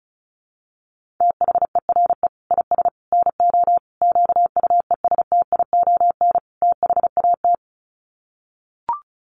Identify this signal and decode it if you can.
Morse code, heard as a keyed tone, about 34 wpm, 700 Hz. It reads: THERE IS NO QUESTION THAT
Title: Sets of 5 Words Encore 35wpm